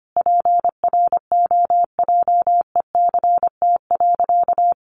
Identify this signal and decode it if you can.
Morse code, heard as a keyed tone, about 25 wpm, 700 Hz. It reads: PROJECT.
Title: Sentences from Top 1400 Words 25wpm